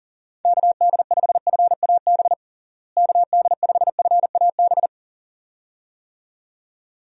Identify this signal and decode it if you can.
Morse code, heard as a keyed tone, about 37 wpm, 700 Hz. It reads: KD5FAB KD5FAB
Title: USA Call Signs - 6 Characters - Repeated 2x - 40wpm